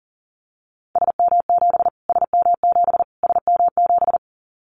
Morse code, heard as a keyed tone, about 40 wpm, 700 Hz. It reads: SM7 SM7 SM7
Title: Callsign Prefixes for Top 25 DXCC - Repeated 3x - 40wpm